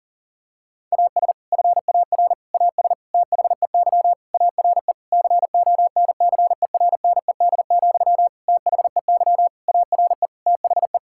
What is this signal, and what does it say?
Morse code, heard as a keyed tone, about 37 wpm, 700 Hz. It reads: AS FAR AS THEY ARE CONCERNED, THEY ARE THE
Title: Sentences from Top 1000 Words - Review Part 6 - 40wpm